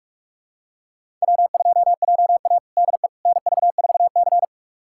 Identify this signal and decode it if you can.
Morse code, heard as a keyed tone, about 46 wpm, 700 Hz. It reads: W2JA DE NV4C
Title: USA Callsign DE Callsign - 4 Characters - 45wpm